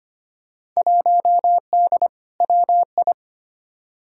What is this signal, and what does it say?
Morse code, heard as a keyed tone, about 25 wpm, 700 Hz. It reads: 1D WI
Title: Field Day Exchanges - 25wpm